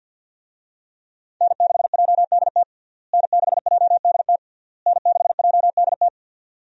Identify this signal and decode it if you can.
Morse code, heard as a keyed tone, about 46 wpm, 700 Hz. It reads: N6JDT N6JDT N6JDT
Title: USA Call Signs - 5 Characters - Repeated 3x - 50wpm